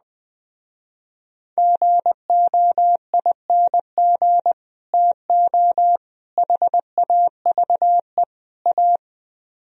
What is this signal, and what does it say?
Morse code, heard as a keyed tone, about 20 wpm, 700 Hz. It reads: GOING TO HAVE A